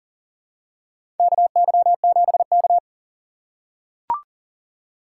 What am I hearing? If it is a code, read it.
Morse code, heard as a keyed tone, about 40 wpm, 700 Hz. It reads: KY7K
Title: K1USN SST QSO Practice - 40wpm